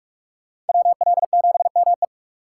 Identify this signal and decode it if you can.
Morse code, heard as a keyed tone, about 56 wpm, 700 Hz. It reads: WL7ME